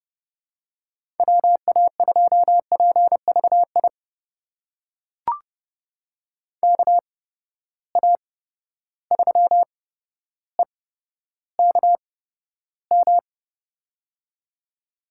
Morse code, heard as a keyed tone, about 32 wpm, spaced as for 13 wpm, 700 Hz. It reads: WA2PVI KA3EKM
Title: USA Call Signs - 6 Characters - 8x Character Spacing - 30wpm